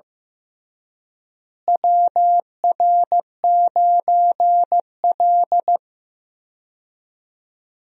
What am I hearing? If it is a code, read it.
Morse code, heard as a keyed tone, about 15 wpm, 700 Hz. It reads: WR9L
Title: USA Call Signs - 4 Characters - Repeated 3x - 15wpm